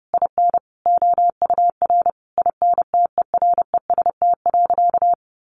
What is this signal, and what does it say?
Morse code, heard as a keyed tone, about 30 wpm, 700 Hz. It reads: IN OUR INTEREST.
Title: Sentences from Top 1300 Words 30wpm